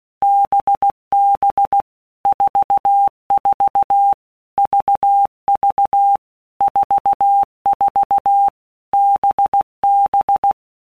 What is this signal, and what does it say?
Morse code, heard as a keyed tone, about 16 wpm, 800 Hz. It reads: BB 44 VV 44 BB
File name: Basic Letter Groups By WPM's / Groups@15 / VB46@15